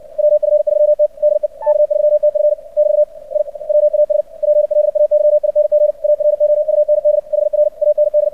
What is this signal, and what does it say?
Morse code, heard as a keyed tone, about 56 wpm, 600 Hz. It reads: WF3T DE K3TF <BT> I WAS PLAYING AROUND HR AND
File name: archived / CRSnet